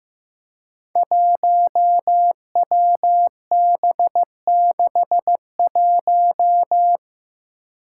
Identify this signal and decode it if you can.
Morse code, heard as a keyed tone, about 15 wpm, 700 Hz. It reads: 1WB61
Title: Five Challenging Letter-Number Combinations 15wpm